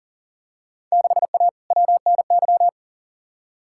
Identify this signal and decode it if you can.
Morse code, heard as a keyed tone, about 40 wpm, 700 Hz. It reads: 6A WNY
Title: Field Day Exchanges - 40wpm